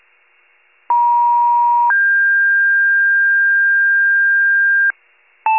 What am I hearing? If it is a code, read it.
Two-tone page, heard as a one-second tone, then a three-second tone, about 950 then 1600 Hz, with noise. It is heard from 0.9 to 4.9 s.
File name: wcalert